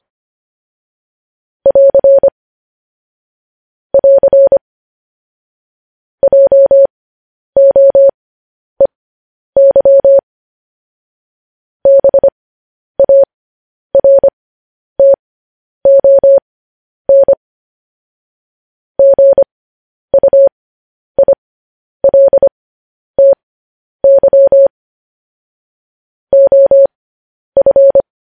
Morse code, heard as a keyed tone, about 25 wpm, spaced as for 10 wpm, 550 Hz. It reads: <AR> <AR> JOEY BARTON GUILTY OF
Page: news-headlines-in-morse-code-at-10-wpm